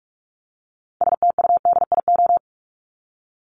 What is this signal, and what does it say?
Morse code, heard as a keyed tone, about 44 wpm, 700 Hz. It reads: STUDIO
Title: Top 1001-1250 Words 45wpm